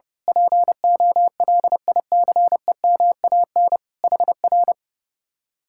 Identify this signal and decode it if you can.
Morse code, heard as a keyed tone, about 28 wpm, 700 Hz. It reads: POLICEMAN HR
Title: Speed-Racing Common QSO Phrases Encore - 30wpm